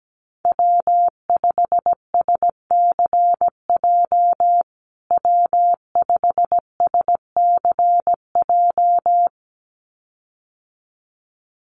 Morse code, heard as a keyed tone, about 17 wpm, 700 Hz. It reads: W5SCJ W5SCJ
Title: USA Call Signs Repeated 2x 17wpm